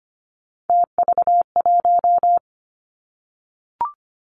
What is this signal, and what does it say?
Morse code, heard as a keyed tone, about 25 wpm, 700 Hz. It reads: TV1
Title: Three Challenging Letter-Number Combinations 25wpm